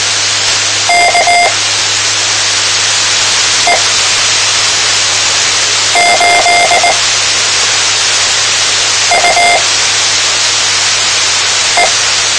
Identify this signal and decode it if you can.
Morse code, heard as a keyed tone, about 19 wpm, spaced as for 4 wpm, 700 Hz, with radio noise. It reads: KT8UT
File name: tmranets / Elmer 04 19 2015